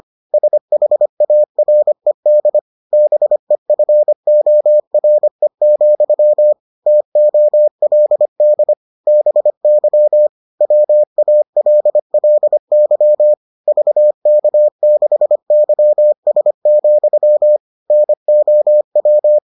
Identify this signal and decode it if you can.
Morse code, heard as a keyed tone, about 25 wpm, 600 Hz. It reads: SHARED BEFORE, TOLD BY WALLY VK6YS, NOW